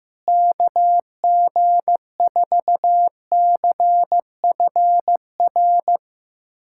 Morse code, heard as a keyed tone, about 15 wpm, 700 Hz. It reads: KG4CFR
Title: USA Callsign DE Callsign - 6 Characters - 15wpm